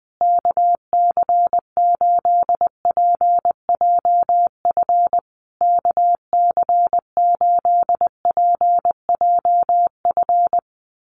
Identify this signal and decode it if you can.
Morse code, heard as a keyed tone, about 20 wpm, 700 Hz. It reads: KC8PJF KC8PJF
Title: USA Call Signs Repeated 3x 20wpm